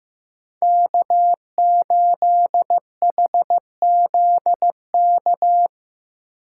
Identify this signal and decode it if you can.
Morse code, heard as a keyed tone, about 15 wpm, 700 Hz. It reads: K8HZK
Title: USA Call Signs - 8x Character Spacing - 15wpm